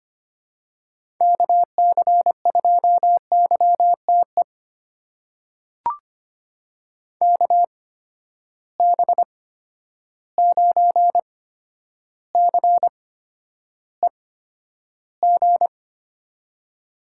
Morse code, heard as a keyed tone, about 25 wpm, spaced as for 11 wpm, 700 Hz. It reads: KC2YTE KB9CEG
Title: USA Call Signs - 8x Character Spacing - 25wpm